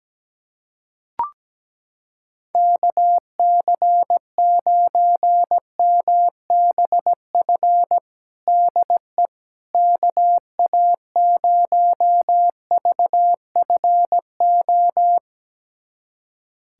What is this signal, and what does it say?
Morse code, heard as a keyed tone, about 17 wpm, 700 Hz. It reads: KC9MBF DE KA0VFO
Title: USA Callsign DE Callsign - 6 Characters - 17wpm